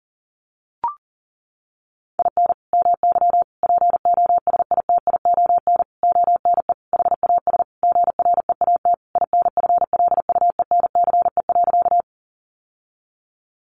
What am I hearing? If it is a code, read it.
Morse code, heard as a keyed tone, about 40 wpm, 700 Hz. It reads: IN MY POSITION ONE HAS GREAT INFLUENCE.
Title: Sentences from Top 1100 Words 40wpm